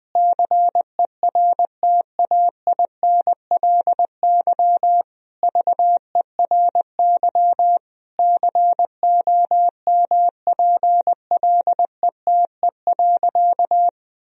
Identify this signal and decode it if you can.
Morse code, heard as a keyed tone, about 20 wpm, 700 Hz. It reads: CERTAINLY VERY COMPLETE.
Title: Sentences from Top 1000 Words - Review Part 2 - 20wpm